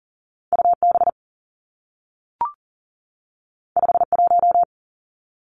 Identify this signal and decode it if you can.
Morse code, heard as a keyed tone, about 40 wpm, 700 Hz. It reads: U B 5 1
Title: Two Challenging Letter-Number Combinations 40wpm